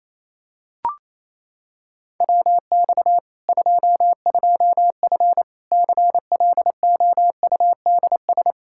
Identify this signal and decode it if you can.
Morse code, heard as a keyed tone, about 28 wpm, 700 Hz. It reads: WX 22F CLOUDS
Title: Practice QSO - Exchange 2 - Info - 28wpm